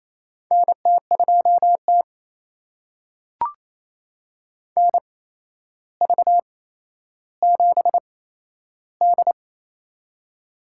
Morse code, heard as a keyed tone, about 28 wpm, spaced as for 12 wpm, 700 Hz. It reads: NT2T NV7D
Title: USA Call Signs - 4 Characters - 8x Character Spacing - 28wpm